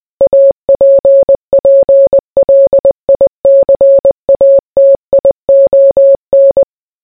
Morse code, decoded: APPLICATION